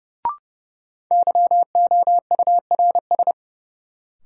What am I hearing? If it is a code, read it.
Morse code, heard as a keyed tone, about 30 wpm, 700 Hz. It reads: YOURS
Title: Sentences from Top 1500 Words - New Word Drill Alt - 30wpm